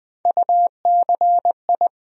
Morse code, heard as a keyed tone, about 20 wpm, 700 Hz. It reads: UCI
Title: USA Call Sign Postfixes - 3 Letters 20wpm